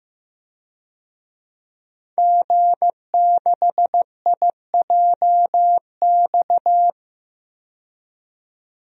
Morse code, heard as a keyed tone, about 15 wpm, 700 Hz. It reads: G6IJX
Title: DXCC-100 Call Signs - 5 Characters - 15wpm